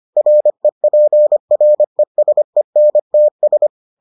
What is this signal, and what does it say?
Morse code, heard as a keyed tone, about 25 wpm, 600 Hz. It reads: REPRESENTS